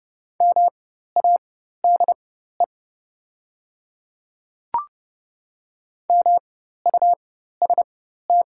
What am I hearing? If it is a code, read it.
Morse code, heard as a keyed tone, about 29 wpm, spaced as for 14 wpm, 700 Hz. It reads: MADE MUST
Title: 4x Character Spacing - Top 300 Words - 30wpm